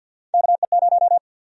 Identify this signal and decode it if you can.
Morse code, heard as a keyed tone, about 52 wpm, 700 Hz. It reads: KE0
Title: USA Call Sign Prefixes - 50wpm